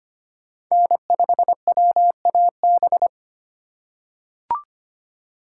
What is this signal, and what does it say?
Morse code, heard as a keyed tone, about 25 wpm, 700 Hz. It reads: N5WAB
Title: USA Call Signs - 5 Characters - 3x Character Spacing - 25wpm